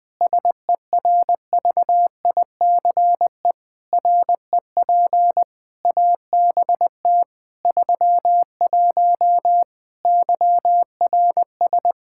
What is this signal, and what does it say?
Morse code, heard as a keyed tone, about 20 wpm, 700 Hz. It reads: SERVICE REP ABT 31 YRS
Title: Practice QSO - Exchange 3 - More Info - 20wpm